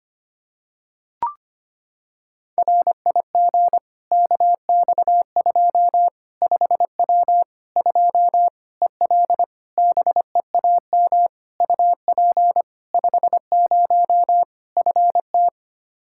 Morse code, heard as a keyed tone, about 25 wpm, 700 Hz. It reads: RIG KX2 5W 2 EL BEAM UP 50 FT
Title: Practice QSO - Exchange 2 - Info - 25wpm